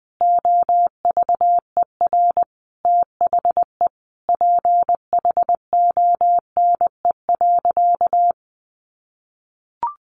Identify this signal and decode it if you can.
Morse code, heard as a keyed tone, about 20 wpm, 700 Hz. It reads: OVER THE PHONE.
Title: Sentences from Top 1100 Words 20wpm